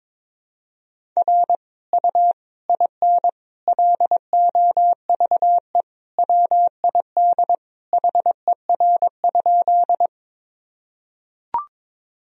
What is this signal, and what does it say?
Morse code, heard as a keyed tone, about 22 wpm, 700 Hz. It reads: R U IN LOVE WID HER?
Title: Sentences from Top 400 Words and Abbreviations 22wpm